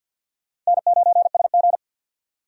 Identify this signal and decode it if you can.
Morse code, heard as a keyed tone, about 52 wpm, 700 Hz. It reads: N9SG
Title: USA Call Signs - 4 Characters - Repeated 3x - 50wpm